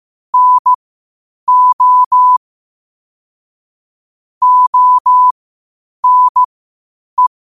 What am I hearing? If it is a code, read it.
Morse code, heard as a keyed tone, about 15 wpm, spaced as for 8 wpm, 1000 Hz. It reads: NO ONE
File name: 8wpm